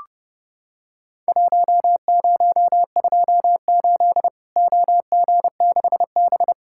Morse code, heard as a keyed tone, about 29 wpm, 700 Hz. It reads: 1028 OG6B
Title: Worked All Europe DX Contest - QTC Exchange - 30wpm